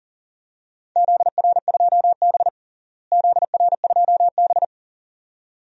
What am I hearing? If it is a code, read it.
Morse code, heard as a keyed tone, about 40 wpm, 700 Hz. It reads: ZR2B ZR2B
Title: DXCC-100 Call Signs - 4 Characters - Repeated 2x - 40wpm